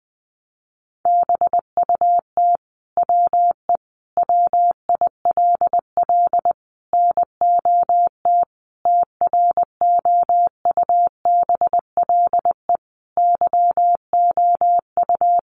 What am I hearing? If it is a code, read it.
Morse code, heard as a keyed tone, about 20 wpm, 700 Hz. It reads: BUT WE WILL NOT TROUBLE YOU
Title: Sentences from Top 1200 Words 20wpm